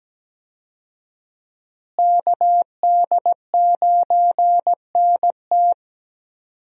Morse code, heard as a keyed tone, about 17 wpm, 700 Hz. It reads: KD9NT
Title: USA Call Signs - 5 Characters - Repeated 2x - 17wpm